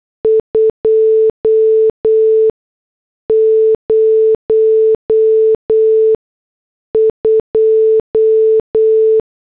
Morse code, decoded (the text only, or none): 202